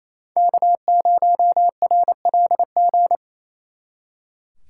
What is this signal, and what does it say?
Morse code, heard as a keyed tone, about 36 wpm, 700 Hz. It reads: K0RLG